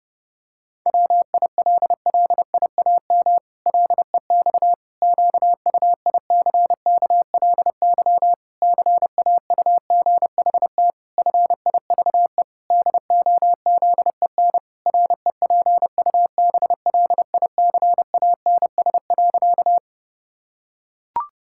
Morse code, heard as a keyed tone, about 30 wpm, 700 Hz. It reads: WILLIAM LEX QUICKLY CAUGHT FIVE DOZEN REPUBLICANS.
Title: Pangram Challenge v2 - 30wpm